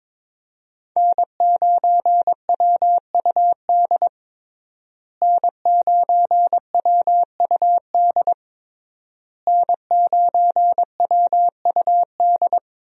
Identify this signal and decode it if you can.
Morse code, heard as a keyed tone, about 22 wpm, 700 Hz. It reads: N9WUD N9WUD N9WUD
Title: USA Call Signs - 5 Characters - Repeated 3x - 22wpm